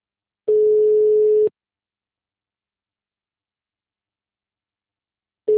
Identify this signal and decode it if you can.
Ringback tone (Europe, 425 Hz).